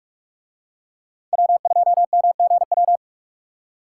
Morse code, heard as a keyed tone, about 40 wpm, 700 Hz. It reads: W2MGW